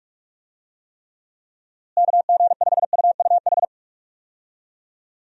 Morse code, heard as a keyed tone, about 46 wpm, 700 Hz. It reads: KG5UUH
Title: USA Call Signs - 6 Characters - Repeated 3x - 45wpm